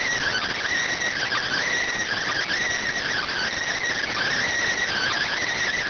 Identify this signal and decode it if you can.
Emergency Alert System (SAME) tones (header bursts), heard as harsh buzzing bursts. It is heard from 1.1 to 5.5 s.